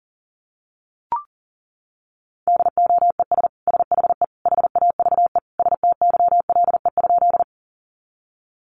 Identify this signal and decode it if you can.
Morse code, heard as a keyed tone, about 40 wpm, 700 Hz. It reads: DOES SHE HAVE STYLE?
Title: Sentences from Top 800 Words 40wpm